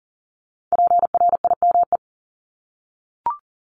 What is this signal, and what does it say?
Morse code, heard as a keyed tone, about 38 wpm, 700 Hz. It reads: PRIME
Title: Multi-letter Phonemes 40wpm